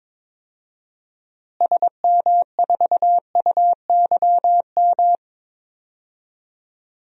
Morse code, heard as a keyed tone, about 22 wpm, 700 Hz. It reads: SM4UYM
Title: DXCC-100 Call Signs - 6 Characters - Repeated 2x - 22wpm